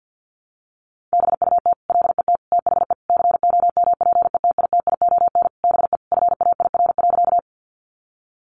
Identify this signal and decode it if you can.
Morse code, heard as a keyed tone, about 50 wpm, 700 Hz. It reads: BUT LET THE COMPETITION BE FAIR.
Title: Sentences from Top 1300 Words 50wpm